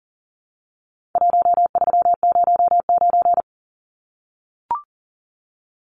Morse code, heard as a keyed tone, about 40 wpm, 700 Hz. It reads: 1309
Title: Numbers - 4-Digit 40wpm